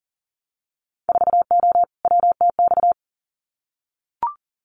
Morse code, heard as a keyed tone, about 38 wpm, 700 Hz. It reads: 4O WTX
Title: Winter Field Day Exchanges Encore 40wpm